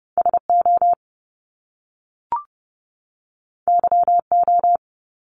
Morse code, heard as a keyed tone, about 30 wpm, 700 Hz. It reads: SO YO
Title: Top Two Letter Combinations in Words 30wpm